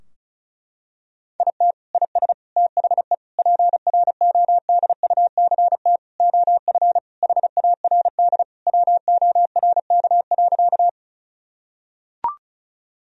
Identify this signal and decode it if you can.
Morse code, heard as a keyed tone, about 34 wpm, 700 Hz. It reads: IT IS THE PRODUCT OF HARD WORK.
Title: Sentences from Top 1000 Words - Review Part 2 - 35wpm